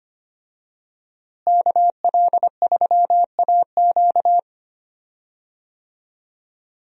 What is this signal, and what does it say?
Morse code, heard as a keyed tone, about 25 wpm, 700 Hz. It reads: KL3AQ
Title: USA Call Signs - 5 Characters - Repeated 3x - 25wpm